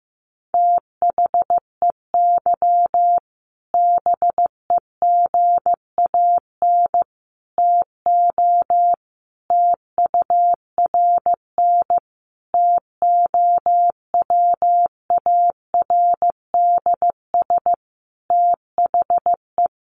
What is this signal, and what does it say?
Morse code, heard as a keyed tone, about 15 wpm, 700 Hz. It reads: THEY BEGAN TO TURN TOWARDS THE
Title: Sentences from Top 1100 Words 15wpm